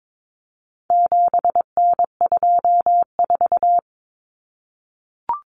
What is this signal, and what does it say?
Morse code, heard as a keyed tone, about 22 wpm, 700 Hz. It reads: 7N24
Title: Common Cut Numbers - 4-Digit 22wpm